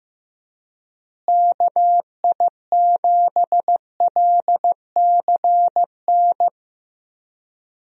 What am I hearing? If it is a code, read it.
Morse code, heard as a keyed tone, about 15 wpm, 700 Hz. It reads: KI7LCN
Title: USA Call Signs - 6 Characters - Repeated 2x - 15wpm